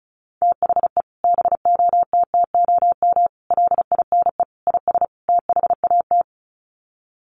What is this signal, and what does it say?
Morse code, heard as a keyed tone, about 34 wpm, 700 Hz. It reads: THE BOTTOM LINE IS THAT
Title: Sets of 5 Words Encore 35wpm